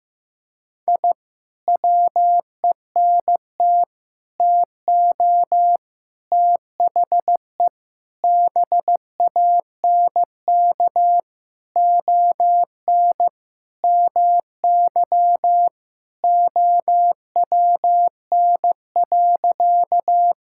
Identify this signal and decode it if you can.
Morse code, heard as a keyed tone, about 15 wpm, 700 Hz. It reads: I WENT TO THE BANK ON MY OWN.
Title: Sentences from Top 1000 Words - Review Part 6 - 15wpm